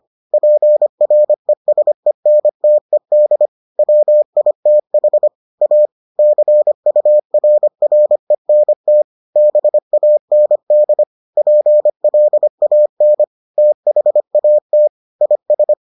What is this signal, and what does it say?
Morse code, heard as a keyed tone, about 25 wpm, 600 Hz. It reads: PRESENTED WITH A CURRENT BAND PLAN THAT IS